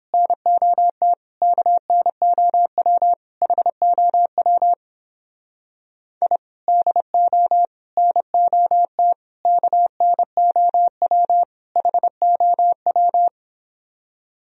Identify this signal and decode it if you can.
Morse code, heard as a keyed tone, about 29 wpm, 700 Hz. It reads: NOT KNOW HOW I DO NOT KNOW HOW